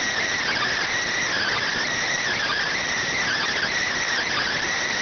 Emergency Alert System (SAME) tones, heard as harsh buzzing bursts. It is heard from 1.1 to 3.9 s.